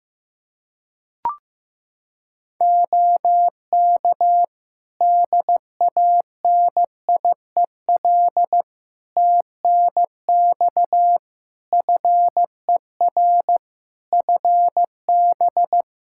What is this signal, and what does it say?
Morse code, heard as a keyed tone, about 16 wpm, 700 Hz. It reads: OK DANIEL TNX FER FB
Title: Practice QSO - Exchange 4 - Ending - 15wpm